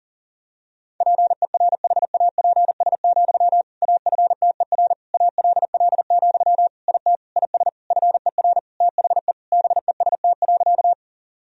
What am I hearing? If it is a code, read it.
Morse code, heard as a keyed tone, about 40 wpm, 700 Hz. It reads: PERHAPS, AFTER ALL, IT IS FER THE BEST.
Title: Sentences from Top 500 Words and Abbreviations 40wpm